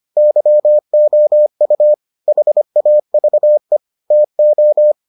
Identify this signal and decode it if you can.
Morse code, heard as a keyed tone, about 25 wpm, 600 Hz. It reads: YOU HAVE TO